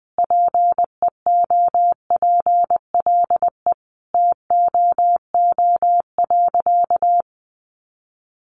Morse code, heard as a keyed tone, about 20 wpm, 700 Hz. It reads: PEOPLE TOO.
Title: Sentences from Top 1000 Words 20wpm